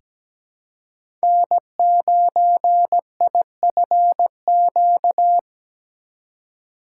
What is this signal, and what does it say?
Morse code, heard as a keyed tone, about 17 wpm, 700 Hz. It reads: N9IFQ
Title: USA Call Signs - 5 Characters - 3x Character Spacing - 17wpm